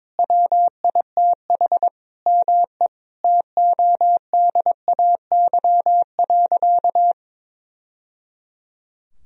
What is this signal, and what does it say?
Morse code, heard as a keyed tone, about 22 wpm, 700 Hz. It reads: WITH ME TODAY.
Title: Sentences from Top 1000 Words - Review Part 4 - 22wpm